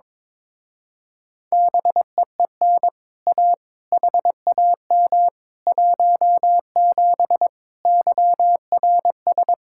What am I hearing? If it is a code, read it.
Morse code, heard as a keyed tone, about 23 wpm, 700 Hz. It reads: BEEN A HAM 17 YRS
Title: Speed-Racing Common QSO Phrases - 22wpm